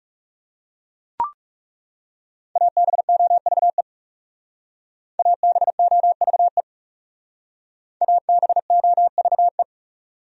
Morse code, heard as a keyed tone, about 37 wpm, 700 Hz. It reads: ABOVE ABOVE ABOVE
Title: Speed-Racing Top 500 Words 30wpm